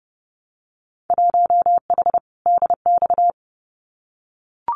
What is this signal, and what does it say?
Morse code, heard as a keyed tone, about 30 wpm, 700 Hz. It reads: 1H DX
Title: Winter Field Day Exchanges 30wpm